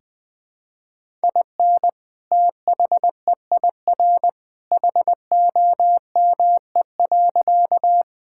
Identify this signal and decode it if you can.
Morse code, heard as a keyed tone, about 20 wpm, 700 Hz. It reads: IN THEIR HOME.
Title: Sentences with 2-Word Phrases - 20wpm